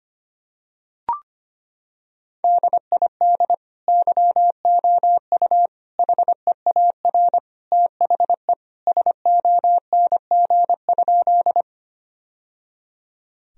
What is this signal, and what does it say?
Morse code, heard as a keyed tone, about 25 wpm, 700 Hz. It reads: DID YOU HEAR THE SONG?
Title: Sentences from Top 1000 Words - Review Part 2 - 25wpm